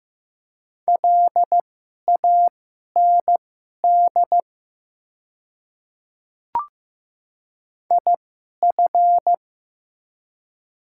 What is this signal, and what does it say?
Morse code, heard as a keyed tone, about 15 wpm, spaced as for 11 wpm, 700 Hz. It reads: LAND IF
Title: 2x Character Spacing - Top 200 Words - 15wpm